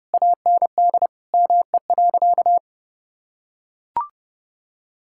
Morse code, heard as a keyed tone, about 28 wpm, 700 Hz. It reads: AND ME.
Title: Sentences from Top 1600 Words 30wpm